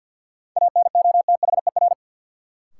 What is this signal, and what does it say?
Morse code, heard as a keyed tone, about 52 wpm, 700 Hz. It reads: ANOTHER
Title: Sentences from Top 600 Words - New Word Drill - 50wpm